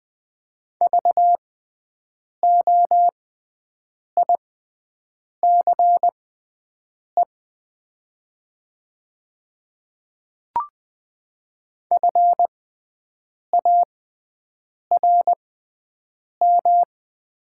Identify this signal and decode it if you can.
Morse code, heard as a keyed tone, about 20 wpm, spaced as for 7 wpm, 700 Hz. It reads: VOICE FARM
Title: 6x Character Spacing - Top 500 Words - 20wpm